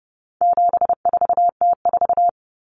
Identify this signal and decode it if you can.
Morse code, heard as a keyed tone, about 30 wpm, 700 Hz. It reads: 74T4
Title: Common Cut Numbers - 4-Digit 30wpm